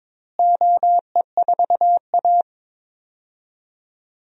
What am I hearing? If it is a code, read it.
Morse code, heard as a keyed tone, about 22 wpm, 700 Hz. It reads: OE4A
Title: DXCC-100 Call Signs - 4 Characters - Repeated 2x - 22wpm